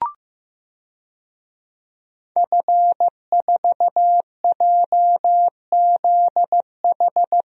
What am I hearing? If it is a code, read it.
Morse code, heard as a keyed tone, about 15 wpm, 700 Hz. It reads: F4JZH
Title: DXCC-100 Call Signs - 5 Characters - Repeated 2x - 15wpm